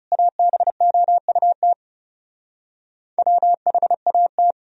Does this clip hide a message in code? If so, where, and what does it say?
Yes: Morse code from 0.1 to 4.5 s, heard as a keyed tone, about 40 wpm, 700 Hz. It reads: ABOUT WHAT